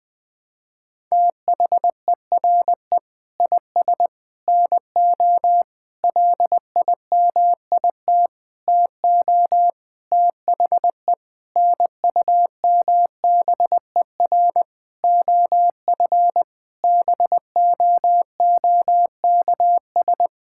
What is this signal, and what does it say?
Morse code, heard as a keyed tone, about 19 wpm, 700 Hz. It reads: THERE IS NO LIMIT TO THE NUMBER OF BOOKS
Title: Sentences with 8-Word Phrases - 20wpm